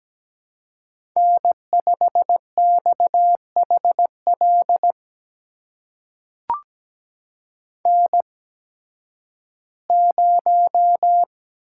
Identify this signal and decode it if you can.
Morse code, heard as a keyed tone, about 17 wpm, spaced as for 11 wpm, 700 Hz. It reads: N5XHL N0
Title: USA Call Signs - 5 Characters - 8x Character Spacing - 17wpm